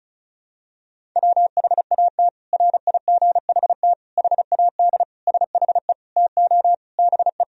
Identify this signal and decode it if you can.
Morse code, heard as a keyed tone, about 36 wpm, 700 Hz. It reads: WHAT RIGHT HAD SHE TO BE
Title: Sentences from Top 1000 Words - Review Part 1 - 35wpm